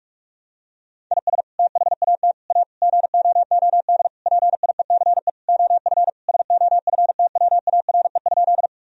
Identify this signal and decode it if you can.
Morse code, heard as a keyed tone, about 46 wpm, 700 Hz. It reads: IS THAT A GOOD PIECE OF SOFTWARE?
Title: Sentences from Top 1000 Words - Review Part 3 - 45wpm